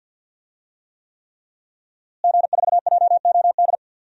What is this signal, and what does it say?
Morse code, heard as a keyed tone, about 46 wpm, 700 Hz. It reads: G4JOD
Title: DXCC-100 Call Signs - 5 Characters - Repeated 3x - 50wpm